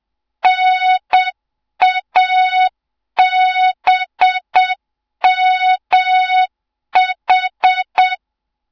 Morse code, heard as a keyed tone, about 7 wpm, 3700 Hz. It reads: NABMH
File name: Morze-HA2MN / hang / 07 GYAKORLAT